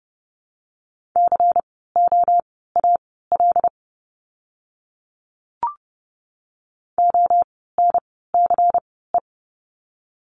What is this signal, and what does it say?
Morse code, heard as a keyed tone, about 32 wpm, spaced as for 17 wpm, 700 Hz. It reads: COAL ONCE
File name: Lesson 16f - Letter c - Words 30wpm